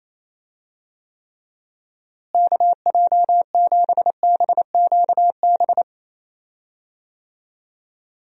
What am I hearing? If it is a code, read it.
Morse code, heard as a keyed tone, about 28 wpm, 700 Hz. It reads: KJ7BQB
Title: USA Call Signs - 6 Characters - Repeated 3x - 28wpm